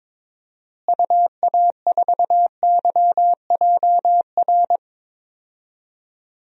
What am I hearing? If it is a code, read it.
Morse code, heard as a keyed tone, about 22 wpm, 700 Hz. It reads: UA4YJR
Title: DXCC-100 Call Signs - 6 Characters - Repeated 3x - 22wpm